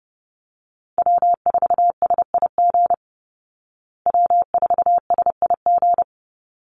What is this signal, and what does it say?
Morse code, heard as a keyed tone, about 30 wpm, 700 Hz. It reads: W4SIG W4SIG
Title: Call Signs Repeated 3x - 30wpm